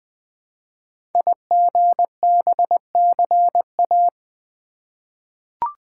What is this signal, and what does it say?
Morse code, heard as a keyed tone, about 20 wpm, 700 Hz. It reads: IGBCA
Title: Code Groups - 5 Letters - 20wpm